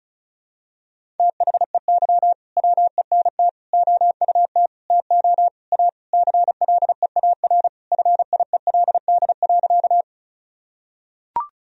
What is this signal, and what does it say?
Morse code, heard as a keyed tone, about 36 wpm, 700 Hz. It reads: THEY WENT OUT TO A CLEAR FIELD.
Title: Sentences from Top 1000 Words - Review Part 2 - 35wpm